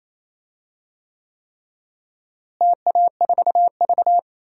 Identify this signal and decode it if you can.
Morse code, heard as a keyed tone, about 28 wpm, 700 Hz. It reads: TA4V
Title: DXCC-100 Call Signs - 4 Characters - Repeated 3x - 28wpm